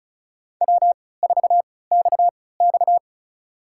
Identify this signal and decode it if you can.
Morse code, heard as a keyed tone, about 36 wpm, spaced as for 20 wpm, 700 Hz. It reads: W4XX
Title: USA Call Signs - 3x Character Spacing - 35wpm